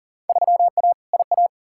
Morse code, heard as a keyed tone, about 40 wpm, 700 Hz. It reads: 3A IA
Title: Field Day Exchanges Encore - 40wpm